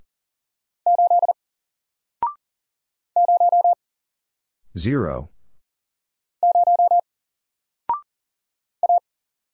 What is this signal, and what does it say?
Morse code, heard as a keyed tone, about 40 wpm, 700 Hz. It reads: 8 0 0 A
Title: Single Character (Letter:Number:Punctuation:Prosign) Alt 40wpm